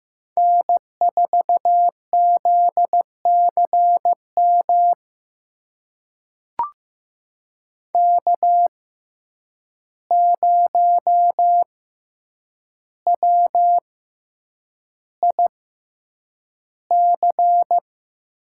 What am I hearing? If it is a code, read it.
Morse code, heard as a keyed tone, about 15 wpm, spaced as for 8 wpm, 700 Hz. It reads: N4ZCM K0WIC
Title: USA Call Signs - 5 Characters - 6x Character Spacing - 15wpm